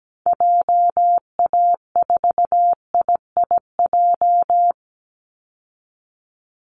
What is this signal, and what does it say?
Morse code, heard as a keyed tone, about 17 wpm, 700 Hz. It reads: JA4IIJ
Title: Call Signs Repeated 3x - 17wpm